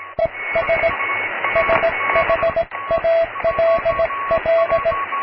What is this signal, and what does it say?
Morse code, heard as a keyed tone, about 17 wpm, 660 Hz, with radio noise. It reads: ES SHALL
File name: cw / tekst(b)+QRM